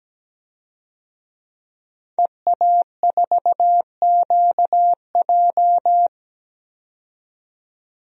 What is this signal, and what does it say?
Morse code, heard as a keyed tone, about 17 wpm, 700 Hz. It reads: EA4QJ
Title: DXCC-100 Call Signs - 5 Characters - 17wpm